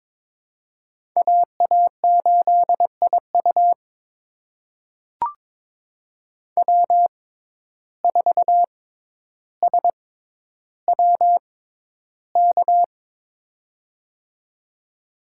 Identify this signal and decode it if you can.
Morse code, heard as a keyed tone, about 22 wpm, spaced as for 11 wpm, 700 Hz. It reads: AA8IU W4SWK
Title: USA Call Signs - 5 Characters - 6x Character Spacing - 22wpm